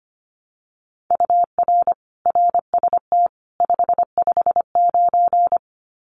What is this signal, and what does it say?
Morse code, heard as a keyed tone, about 25 wpm, 700 Hz. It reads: UR RST 559
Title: Common QSO Phrases 25wpm